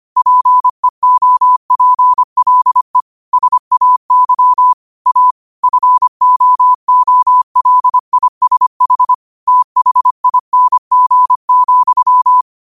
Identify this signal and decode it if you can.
Morse code, heard as a keyed tone, about 25 wpm, 1000 Hz. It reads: PEOPLE SAY A FOOLISH THING,
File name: 25wpm